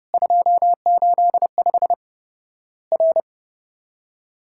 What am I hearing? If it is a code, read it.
Morse code, heard as a keyed tone, about 30 wpm, 700 Hz. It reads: 285 R
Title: Worked All Europe DX Contest - QTC Exchange - 30wpm